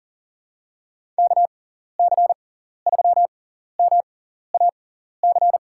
Morse code, heard as a keyed tone, about 40 wpm, spaced as for 14 wpm, 700 Hz. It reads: KC3MAC
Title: USA Call Signs - 6 Characters - 6x Character Spacing - 40wpm